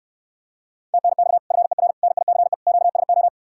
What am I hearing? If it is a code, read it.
Morse code, heard as a keyed tone, about 77 wpm, 700 Hz. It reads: TNX FER NICE QSO